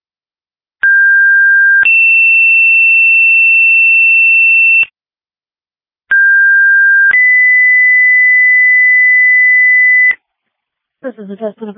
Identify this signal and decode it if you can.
Two-tone page, heard as a one-second tone, then a three-second tone, about 1600 then 2700 Hz, about 1600 then 2000 Hz, with speech. It is heard from 0.8 to 10.1 s.